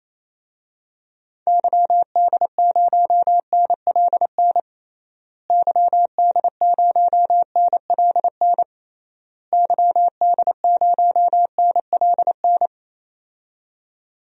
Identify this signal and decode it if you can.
Morse code, heard as a keyed tone, about 28 wpm, 700 Hz. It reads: YD0NLN YD0NLN YD0NLN
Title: DXCC-100 Call Signs - 6 Characters - Repeated 3x - 28wpm